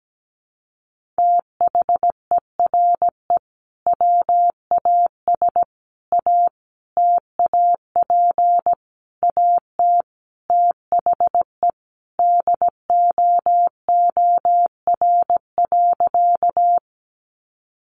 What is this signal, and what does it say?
Morse code, heard as a keyed tone, about 17 wpm, 700 Hz. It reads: THERE WAS A TAP AT THE DOOR.
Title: Sentences from Top 1400 Words 17wpm